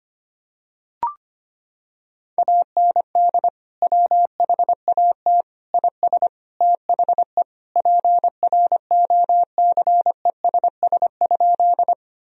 Morse code, heard as a keyed tone, about 25 wpm, 700 Hz. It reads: AND WHAT IS THE PROCESS?
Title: Sentences from Top 1000 Words - Review Part 3 - 25wpm